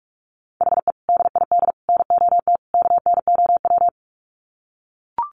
Morse code, heard as a keyed tone, about 44 wpm, 700 Hz. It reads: HE DID NOT KNOW
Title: Sets of 4 Words 45wpm